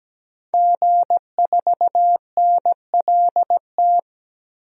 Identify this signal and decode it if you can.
Morse code, heard as a keyed tone, about 17 wpm, 700 Hz. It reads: G4NLT
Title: DXCC-100 Call Signs - 5 Characters - 17wpm